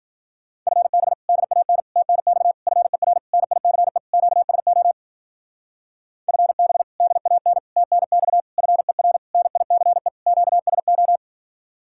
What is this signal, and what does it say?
Morse code, heard as a keyed tone, about 49 wpm, 700 Hz. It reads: FB DAN TNX FER NICE QSO FB DAN TNX FER NICE QSO
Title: Speed-Racing Common QSO Phrases - 40wpm